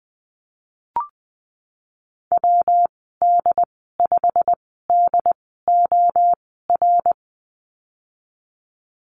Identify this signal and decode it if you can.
Morse code, heard as a keyed tone, about 20 wpm, spaced as for 15 wpm, 700 Hz. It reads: WD5DOR
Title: 167 Lesson 27g - Number 7 - Call Signs 20wpm